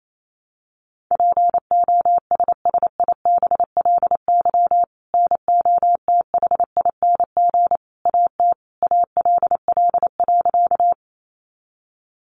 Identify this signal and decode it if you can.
Morse code, heard as a keyed tone, about 28 wpm, 700 Hz. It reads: POSSIBLY NOTHING AT ALL.
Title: Sentences from Top 1300 Words 28wpm